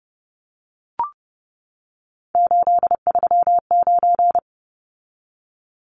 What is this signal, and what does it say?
Morse code, heard as a keyed tone, about 30 wpm, 700 Hz. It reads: 839
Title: Numbers - 3-Digit 30wpm